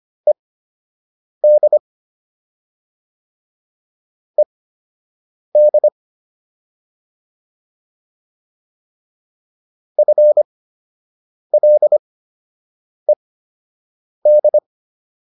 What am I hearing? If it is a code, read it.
Morse code, heard as a keyed tone, about 25 wpm, spaced as for 7 wpm, 600 Hz. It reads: ED ED FLED